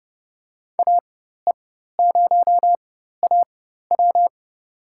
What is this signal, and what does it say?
Morse code, heard as a keyed tone, about 31 wpm, spaced as for 14 wpm, 700 Hz. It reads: AE0AW
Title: USA Call Signs - 5 Characters - 4x Character Spacing - 30wpm